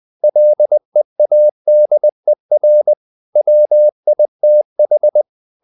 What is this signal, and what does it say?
Morse code, heard as a keyed tone, about 20 wpm, 600 Hz. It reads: LEADER WITH
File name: the old man and the sea part 2 20wpm